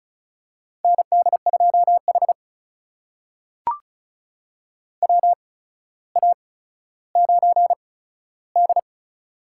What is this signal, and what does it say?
Morse code, heard as a keyed tone, about 34 wpm, spaced as for 15 wpm, 700 Hz. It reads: ND2H WA9D
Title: USA Call Signs - 4 Characters - 8x Character Spacing - 35wpm